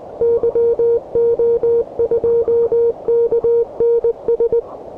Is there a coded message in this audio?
Yes: Morse code from 0.2 to 4.6 s, heard as a keyed tone, about 20 wpm, 470 Hz. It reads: YO2KNS